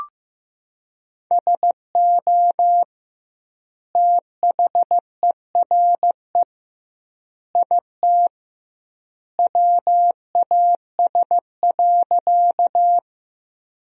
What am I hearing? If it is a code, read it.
Morse code, heard as a keyed tone, about 16 wpm, 700 Hz. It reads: SO THERE IT WAS.
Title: 2x Word-Spacing - Sentences from Top 100 Words - 15wpm